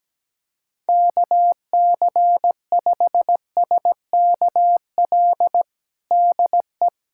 Morse code, heard as a keyed tone, about 17 wpm, 700 Hz. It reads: KC5SKL DE
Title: USA Callsign DE Callsign - 6 Characters - 17wpm